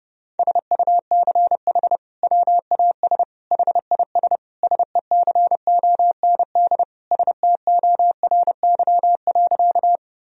Morse code, heard as a keyed tone, about 32 wpm, 700 Hz. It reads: SUCH WAS HIS SECOND STORY.
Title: Sentences from Top 1000 Words - Review Part 1 - 30wpm